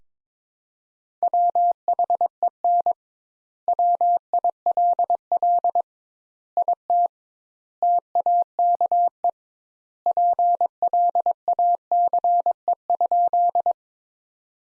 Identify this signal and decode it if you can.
Morse code, heard as a keyed tone, about 21 wpm, 700 Hz. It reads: WHEN WILL IT TAKE PLACE?
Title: Sentences Top 200 Words - 22wpm